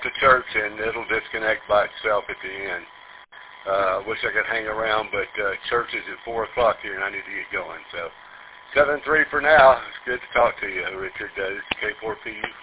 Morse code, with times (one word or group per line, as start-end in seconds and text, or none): none